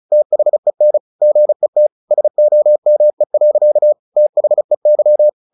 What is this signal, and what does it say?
Morse code, heard as a keyed tone, about 35 wpm, 600 Hz. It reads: THEN GET SOME. THEY
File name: the old man and the sea / the old man and the sea 1 35wpm